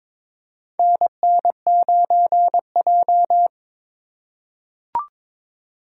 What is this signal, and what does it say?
Morse code, heard as a keyed tone, about 23 wpm, 700 Hz. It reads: NN9J
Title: USA Call Signs - 4 Characters - 2x Character Spacing - 22wpm